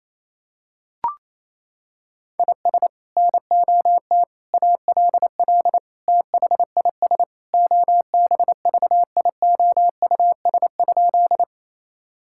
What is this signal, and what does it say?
Morse code, heard as a keyed tone, about 28 wpm, 700 Hz. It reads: IS NOT ALL THIS OBVIOUS?
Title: Sentences from Top 1600 Words 28wpm